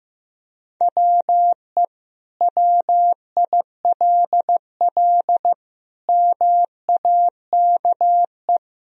Morse code, heard as a keyed tone, about 16 wpm, 700 Hz. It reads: WE WILL MAKE
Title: Sentences from Top 1000 Words - Review Part 1 - 15wpm